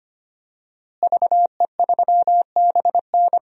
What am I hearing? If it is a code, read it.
Morse code, heard as a keyed tone, about 25 wpm, 700 Hz. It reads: VE3BN
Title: DXCC-100 Call Signs - 5 Characters - Repeated 2x - 25wpm